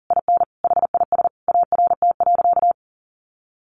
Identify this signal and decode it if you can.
Morse code, heard as a keyed tone, about 40 wpm, 700 Hz. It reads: IN HIS ART.
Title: Sentences from Top 1200 Words 40wpm